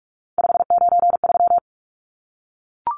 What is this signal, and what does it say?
Morse code, heard as a keyed tone, about 44 wpm, 700 Hz. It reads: 593
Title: Numbers - 3-Digit 45wpm